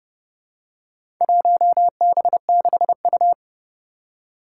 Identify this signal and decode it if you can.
Morse code, heard as a keyed tone, about 30 wpm, 700 Hz. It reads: 1B6U
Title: Four Challenging Letter-Number Combinations 30wpm